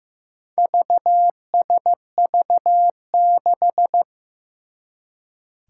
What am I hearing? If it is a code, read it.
Morse code, heard as a keyed tone, about 15 wpm, 700 Hz. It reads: VSV6
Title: Four Challenging Letter-Number Combinations 15wpm